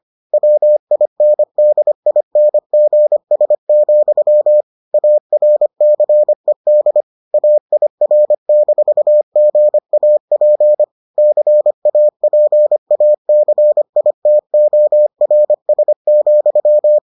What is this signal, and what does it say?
Morse code, heard as a keyed tone, about 25 wpm, 600 Hz. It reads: WINDINGS, ARCED AIR-GAP CAPACITORS,